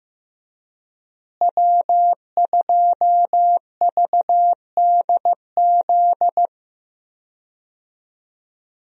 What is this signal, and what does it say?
Morse code, heard as a keyed tone, about 16 wpm, 700 Hz. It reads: W2VDZ
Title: USA Call Signs - 5 Characters - Repeated 2x - 15wpm